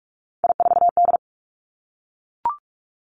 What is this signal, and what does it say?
Morse code, heard as a keyed tone, about 46 wpm, 700 Hz. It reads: I4D
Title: Three Challenging Letter-Number Combinations 45wpm